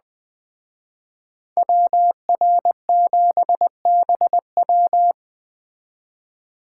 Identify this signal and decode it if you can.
Morse code, heard as a keyed tone, about 20 wpm, 700 Hz. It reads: WR7BW
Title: Speed Racing - USA 2x2 Call Signs 17wpm